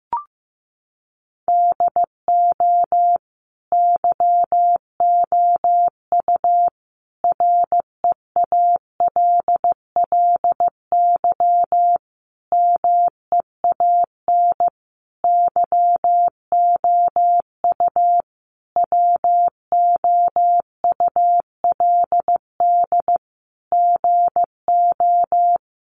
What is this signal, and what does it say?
Morse code, heard as a keyed tone, about 15 wpm, 700 Hz. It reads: DO YOU REALLY MEAN YOU WOULD GO
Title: Sentences from Top 600 Words 15wpm